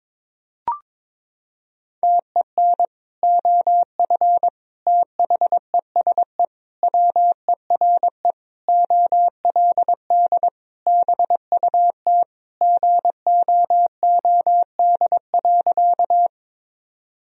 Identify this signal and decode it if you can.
Morse code, heard as a keyed tone, about 22 wpm, 700 Hz. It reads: TEN OF THESE WERE OLD BUT GOOD.
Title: Sentences from Top 1000 Words - Review Part 2 - 22wpm